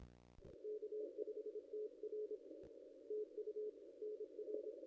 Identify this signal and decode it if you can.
Morse code, heard as a keyed tone, about 29 wpm, 400 Hz. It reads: K5TR TU NR
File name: contests / ss-cw-k5tr-op-n5rz -0600